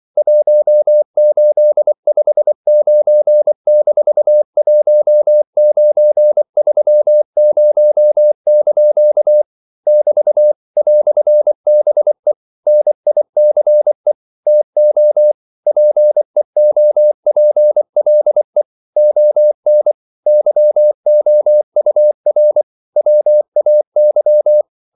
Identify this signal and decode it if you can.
Morse code, heard as a keyed tone, about 24 wpm, 600 Hz. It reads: 1859-1930) <BT> "BE NICE TO PEOPLE ON YOUR WAY
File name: CW / Practice / EnglishQuotations0224